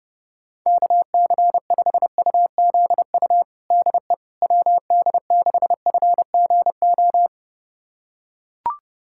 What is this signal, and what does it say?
Morse code, heard as a keyed tone, about 32 wpm, 700 Hz. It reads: KC5UZU DE WD6FGO
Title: USA Callsign DE Callsign 30wpm